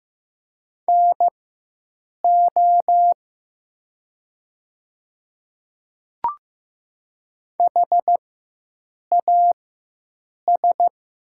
Morse code, heard as a keyed tone, about 15 wpm, spaced as for 7 wpm, 700 Hz. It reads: NO HAS
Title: 4x Character Spacing - Top 100 Words - 15wpm